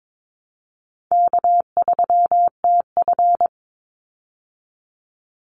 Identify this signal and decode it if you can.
Morse code, heard as a keyed tone, about 22 wpm, 700 Hz. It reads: K3TF
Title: Easy Call Signs Repeated 3x - 22wpm